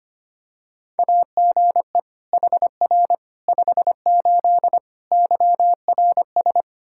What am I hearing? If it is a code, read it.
Morse code, heard as a keyed tone, about 25 wpm, 700 Hz. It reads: AGE HR 58 YRS
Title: Practice QSO - Exchange 3 - More Info - 25wpm